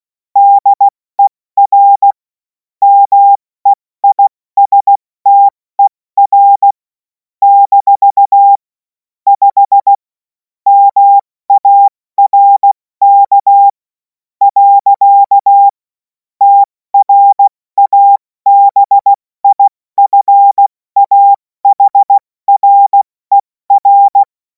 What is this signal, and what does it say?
Morse code, heard as a keyed tone, about 16 wpm, spaced as for 14 wpm, 800 Hz. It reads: DER MEISTER - 5 MARK. TRABIFAHRER
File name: Morsekurs / Witze / Witz 21. April 2026 70 F80